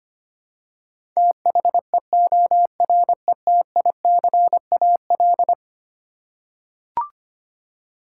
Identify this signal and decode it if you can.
Morse code, heard as a keyed tone, about 25 wpm, 700 Hz. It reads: THEORETICAL
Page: morse-code-ninja-practice-25wpm